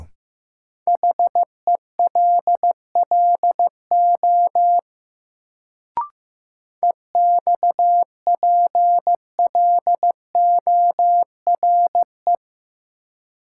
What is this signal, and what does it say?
Morse code, heard as a keyed tone, about 15 wpm, 700 Hz. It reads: HELLO EXPLORE
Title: Sentences from Top 1100 Words - New Word Drill Alt - 15wpm